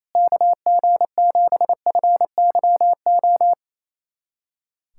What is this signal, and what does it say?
Morse code, heard as a keyed tone, about 28 wpm, 700 Hz. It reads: KG7FYO
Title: USA Callsign DE Callsign 28wpm